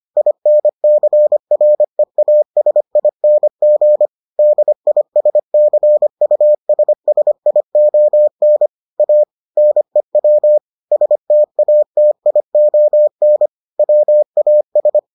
Morse code, heard as a keyed tone, about 25 wpm, 600 Hz. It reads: INCREASING DISCUSSION A NEW STATION WAS